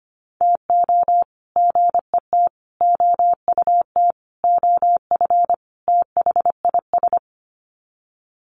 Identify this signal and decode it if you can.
Morse code, heard as a keyed tone, about 25 wpm, 700 Hz. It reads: TO GET OUT OF THIS
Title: Sets of 5 Words Encore2 25wpm